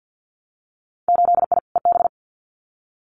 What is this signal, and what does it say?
Morse code, heard as a keyed tone, about 50 wpm, 700 Hz. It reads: 8I EB
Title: Winter Field Day Exchanges Encore 50wpm